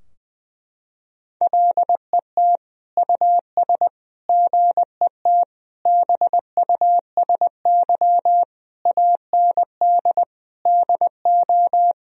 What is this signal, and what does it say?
Morse code, heard as a keyed tone, about 20 wpm, 700 Hz. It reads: LET US GET BUSY AND DO
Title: Sentences from Top 1000 Words - Review Part 3 - 20wpm